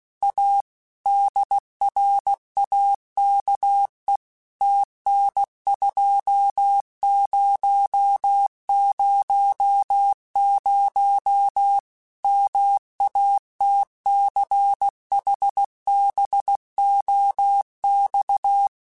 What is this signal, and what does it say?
Morse code, heard as a keyed tone, about 16 wpm, 800 Hz. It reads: A DRAKE TN2000 MATCHBOX
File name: QSOs / QSOs-mp3s / WD0FBT 1@15